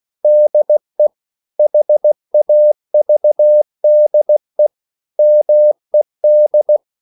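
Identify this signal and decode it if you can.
Morse code, heard as a keyed tone, about 16 wpm, 600 Hz. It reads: DE HAVDE MED
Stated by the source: Slow (QRS) Morse code at 16 wpm, from CW med Gnister.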